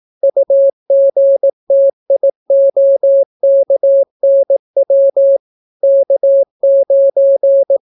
Morse code, heard as a keyed tone, about 18 wpm, 550 Hz. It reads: UGTIOKNW K9